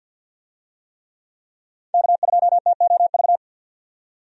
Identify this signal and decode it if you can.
Morse code, heard as a keyed tone, about 52 wpm, 700 Hz. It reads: K2TOV
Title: USA Call Signs - 5 Characters - 50wpm